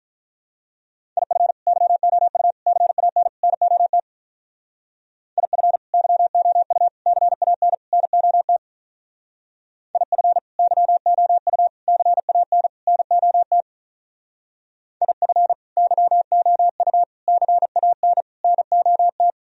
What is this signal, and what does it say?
Morse code, heard as a keyed tone, about 44 wpm, 700 Hz. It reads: IF YOU CAN NOT IF YOU CAN NOT IF YOU CAN NOT IF YOU CAN NOT